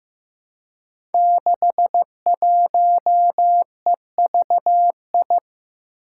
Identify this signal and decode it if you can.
Morse code, heard as a keyed tone, about 15 wpm, 700 Hz. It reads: 61EVI
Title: Five Challenging Letter-Number Combinations 15wpm